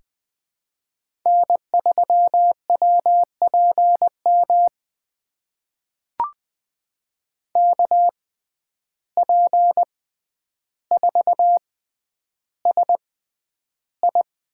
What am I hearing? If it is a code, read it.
Morse code, heard as a keyed tone, about 20 wpm, spaced as for 10 wpm, 700 Hz. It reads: N3WPM KP4SI
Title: USA Call Signs - 5 Characters - 6x Character Spacing - 20wpm